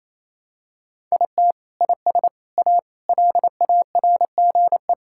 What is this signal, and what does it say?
Morse code, heard as a keyed tone, about 28 wpm, 700 Hz. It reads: IT IS A LARGE
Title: Sentences from Top 1000 Words - Review Part 5 - 28wpm